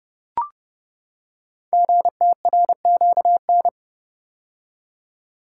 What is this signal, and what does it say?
Morse code, heard as a keyed tone, about 30 wpm, 700 Hz. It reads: GTRQN
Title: Code Groups - 5 Letters - 30wpm